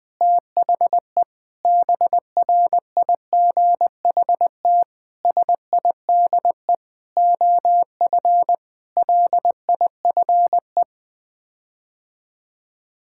Morse code, heard as a keyed tone, about 20 wpm, 700 Hz. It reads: THE BRIGHT SIDE OF LIFE
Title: Sets of 8 Words - 20wpm